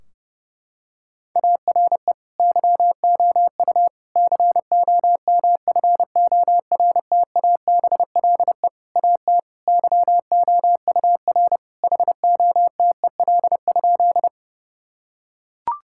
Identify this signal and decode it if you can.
Morse code, heard as a keyed tone, about 30 wpm, 700 Hz. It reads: ARE YOU COMFORTABLE AT YOUR HOTEL?
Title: Sentences from Top 1600 Words 30wpm